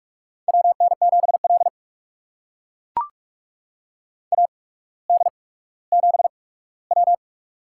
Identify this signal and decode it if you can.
Morse code, heard as a keyed tone, about 45 wpm, spaced as for 19 wpm, 700 Hz. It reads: WN7L AD7W
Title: USA Call Signs - 4 Characters - 8x Character Spacing - 45wpm